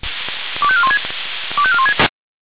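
Five-tone selective call (ZVEI), heard as quick runs of five tones, tones of about 70 ms. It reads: 25416 25416
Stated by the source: ZVEI five-tone selective call received on SDR.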